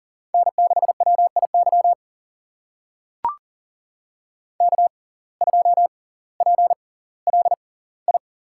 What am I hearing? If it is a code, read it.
Morse code, heard as a keyed tone, about 38 wpm, spaced as for 20 wpm, 700 Hz. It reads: N6WIY K2PLI
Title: USA Call Signs - 5 Characters - 6x Character Spacing - 40wpm